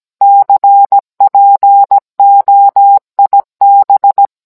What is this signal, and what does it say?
Morse code, heard as a keyed tone, about 17 wpm, 800 Hz. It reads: CPOIB